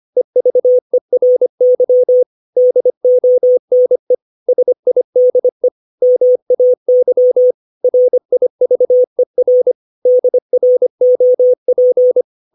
Morse code, heard as a keyed tone, about 25 wpm, 500 Hz. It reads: EVERY DONE SIDE MAY RIVER DROP
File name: 632 words / 632 Words 25WPM